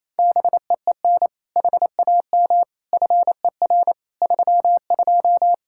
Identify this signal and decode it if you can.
Morse code, heard as a keyed tone, about 28 wpm, 700 Hz. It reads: BEEN HAM FER 32
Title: K1USN SST QSO Practice 28wpm